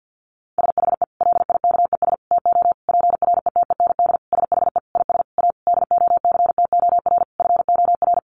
Morse code, heard as a keyed tone, about 50 wpm, 700 Hz. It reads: SHE LIKES TO PRETEND SHE IS A DOCTOR FOR
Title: Additional Easy Phrases 50wpm